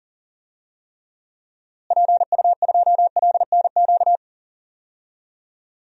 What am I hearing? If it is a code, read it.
Morse code, heard as a keyed tone, about 40 wpm, 700 Hz. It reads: PU2LNQ
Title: DXCC-100 Call Signs - 6 Characters - Repeated 2x - 40wpm